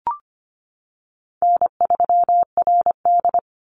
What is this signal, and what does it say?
Morse code, heard as a keyed tone, about 25 wpm, 700 Hz. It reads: N3RD
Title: Easy Call Signs - 25wpm